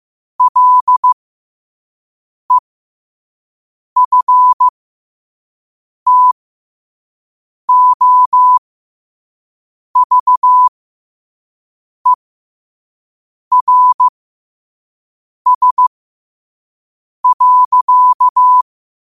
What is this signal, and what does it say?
Morse code, heard as a keyed tone, about 15 wpm, spaced as for 5 wpm, 1000 Hz. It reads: LEFTOVERS.
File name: qotdfiles / 5wpm